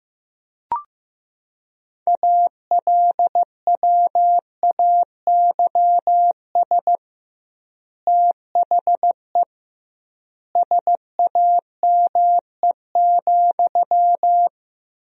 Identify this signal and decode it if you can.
Morse code, heard as a keyed tone, about 15 wpm, 700 Hz. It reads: ALWAYS THE SAME,
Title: 2x Word-Spacing - Sentences from Top 300 Words - 15wpm